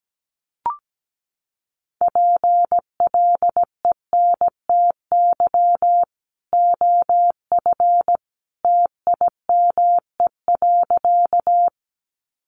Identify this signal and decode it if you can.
Morse code, heard as a keyed tone, about 17 wpm, 700 Hz. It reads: PLENTY OF TIME.
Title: Sentences from Top 1300 Words 17wpm